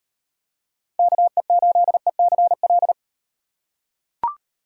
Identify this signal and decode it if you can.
Morse code, heard as a keyed tone, about 38 wpm, 700 Hz. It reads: KE8ECL